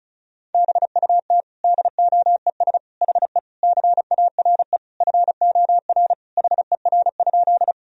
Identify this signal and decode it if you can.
Morse code, heard as a keyed tone, about 36 wpm, 700 Hz. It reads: BUT DOES HE CARE FOR HER?
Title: Sentences from Top 1000 Words - Review Part 1 - 35wpm